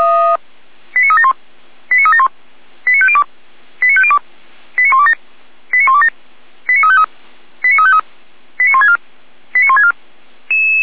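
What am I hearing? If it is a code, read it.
Five-tone selective call (ZVEI), heard as quick runs of five tones, tones of about 70 ms. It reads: 89371 89371 89502 89502 89127 89127 89353 89353 89164 89164
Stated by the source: ZVEI five-tone selective call received on SDR.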